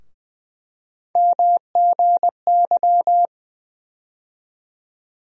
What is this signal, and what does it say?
Morse code, heard as a keyed tone, about 20 wpm, 700 Hz. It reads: MGY
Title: USA Call Sign Postfixes - 3 Letters 20wpm